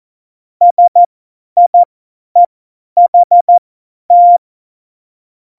Morse code, heard as a keyed tone, about 14 wpm, spaced as for 10 wpm, 700 Hz. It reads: SIEHT
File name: Morsekurs / Witze / Witz 01. September 2020, 50 F70